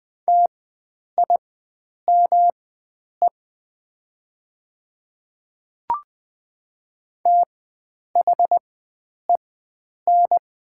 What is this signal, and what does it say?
Morse code, heard as a keyed tone, about 20 wpm, spaced as for 9 wpm, 700 Hz. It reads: TIME THEN
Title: 4x Character Spacing - Top 100 Words - 20wpm